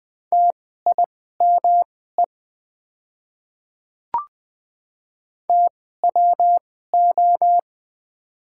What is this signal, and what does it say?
Morse code, heard as a keyed tone, about 20 wpm, spaced as for 14 wpm, 700 Hz. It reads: TIME TWO
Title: 2x Character Spacing - Top 100 Words - 20wpm